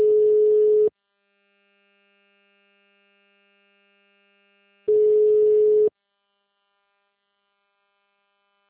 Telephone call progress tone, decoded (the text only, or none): ringback tone